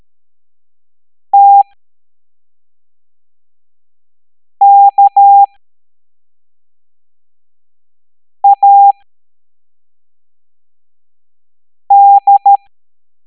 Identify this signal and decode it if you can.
Morse code, heard as a keyed tone, about 13 wpm, spaced as for 3 wpm, 800 Hz. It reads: TKAD